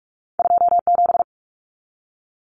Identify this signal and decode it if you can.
Morse code, heard as a keyed tone, about 46 wpm, 700 Hz. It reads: 27
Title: Common Cut Numbers - 2-Digit 45wpm